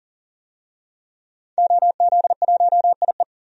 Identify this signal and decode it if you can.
Morse code, heard as a keyed tone, about 40 wpm, 700 Hz. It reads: OZ1IE
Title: DXCC-100 Call Signs - 5 Characters - 40wpm